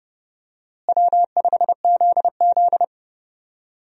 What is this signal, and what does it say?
Morse code, heard as a keyed tone, about 30 wpm, 700 Hz. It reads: W5ZZ
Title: USA Call Signs - 4 Characters - Repeated 3x - 30wpm